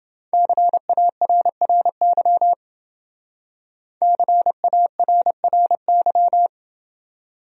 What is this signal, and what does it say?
Morse code, heard as a keyed tone, about 28 wpm, 700 Hz. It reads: CARRY CARRY